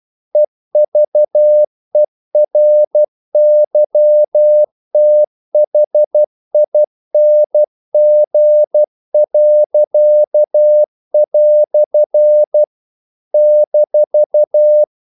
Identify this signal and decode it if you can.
Morse code, heard as a keyed tone, about 12 wpm, 600 Hz. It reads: EVERYTHING." -
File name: CW / Practice / EnglishQuotations0312